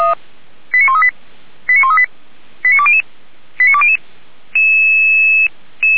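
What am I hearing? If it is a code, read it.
Five-tone selective call (ZVEI), heard as quick runs of five tones, tones of about 70 ms. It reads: 89138 89138 89200 89200
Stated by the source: ZVEI five-tone selective call 89227, received by SDR, from Feuerwehr Orlamünde (fire brigade) paging archive.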